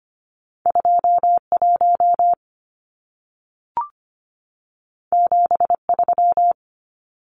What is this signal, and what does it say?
Morse code, heard as a keyed tone, about 25 wpm, 700 Hz. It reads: 21 73
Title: Numbers - 2-Digit 25wpm